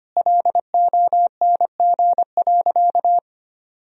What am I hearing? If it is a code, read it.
Morse code, heard as a keyed tone, about 25 wpm, 700 Hz. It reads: LONG.
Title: Sentences from Top 1000 Words - Review Part 2 - 25wpm